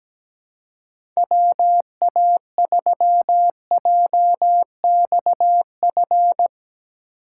Morse code, heard as a keyed tone, about 17 wpm, 700 Hz. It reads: WA3JXF
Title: USA Call Signs - 6 Characters - 3x Character Spacing - 17wpm